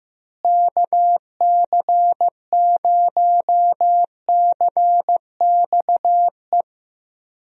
Morse code, heard as a keyed tone, about 15 wpm, 700 Hz. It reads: KC0CXE
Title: USA Call Signs - 6 Characters - Repeated 2x - 15wpm